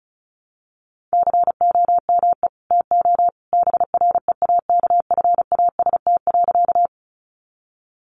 Morse code, heard as a keyed tone, about 34 wpm, 700 Hz. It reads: COME TO BREAKFAST.
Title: Sentences from Top 1500 Words 35wpm